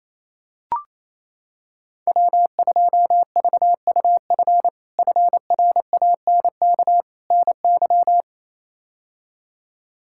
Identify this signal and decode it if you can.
Morse code, heard as a keyed tone, about 27 wpm, 700 Hz. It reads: W2VUF FRANK NY
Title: K1USN SST QSO Practice - 28wpm